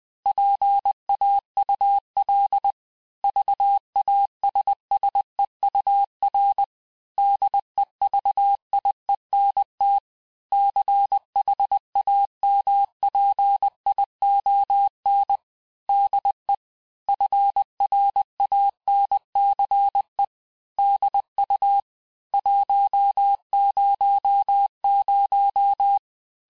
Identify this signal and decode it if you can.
Morse code, heard as a keyed tone, about 20 wpm, 800 Hz. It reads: PAUL VASSEUR DEVIENT CHAMPION DE FRANCE DU 100
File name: cours-du-lundi files / cours-1584-teste-clair-vitesse-19mpm-puis-24mpm